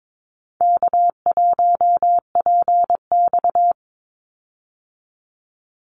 Morse code, heard as a keyed tone, about 22 wpm, 700 Hz. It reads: K1PX
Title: Easy Call Signs Repeated 3x - 22wpm